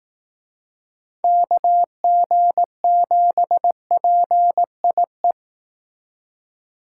Morse code, heard as a keyed tone, about 18 wpm, 700 Hz. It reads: KG7PIE
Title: Speed Racing - USA 2x3 Call Signs 15wpm